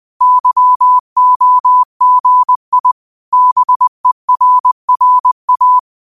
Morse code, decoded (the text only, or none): YOGI BERRA